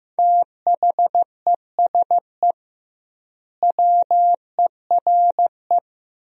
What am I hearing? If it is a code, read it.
Morse code, heard as a keyed tone, about 15 wpm, 700 Hz. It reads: THESE WERE
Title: 2x Word-Spacing - Sentences from Top 300 Words - 15wpm